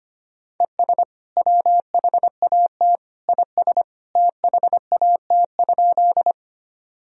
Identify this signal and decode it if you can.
Morse code, heard as a keyed tone, about 25 wpm, 700 Hz. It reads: ES WHAT IS THAT?
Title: Sentences from Top 100 Words and Abbreviations 25wpm